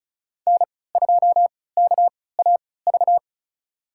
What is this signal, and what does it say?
Morse code, heard as a keyed tone, about 34 wpm, spaced as for 20 wpm, 700 Hz. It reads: N2KAV
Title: USA Call Signs - 5 Characters - 3x Character Spacing - 35wpm